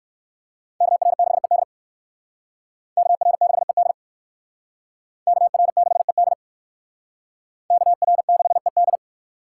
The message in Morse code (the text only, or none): KR6ED KR6ED KR6ED KR6ED